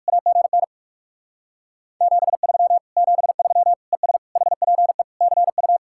Morse code, heard as a keyed tone, about 58 wpm, 700 Hz. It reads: AGN 73 73 ES HPE CU